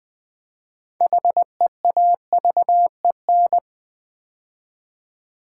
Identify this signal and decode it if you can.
Morse code, heard as a keyed tone, about 20 wpm, 700 Hz. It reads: HEAVEN
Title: Build A Word - 2 Letters At A Time - 6-Letter Words - 20wpm